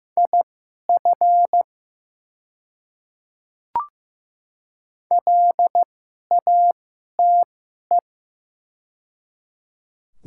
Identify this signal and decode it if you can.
Morse code, heard as a keyed tone, about 15 wpm, spaced as for 11 wpm, 700 Hz. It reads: IF LATE
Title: 2x Character Spacing - Top 300 Words - 15wpm